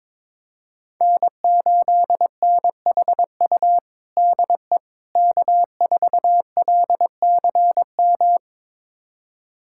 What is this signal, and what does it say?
Morse code, heard as a keyed tone, about 22 wpm, 700 Hz. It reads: N8NHU DE K4LCM
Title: USA Callsign DE Callsign - 5 Characters - 22wpm